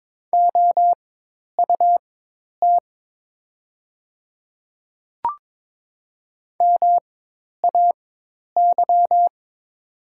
Morse code, heard as a keyed tone, about 22 wpm, spaced as for 10 wpm, 700 Hz. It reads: OUT MAY
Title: 4x Character Spacing - Top 100 Words - 22wpm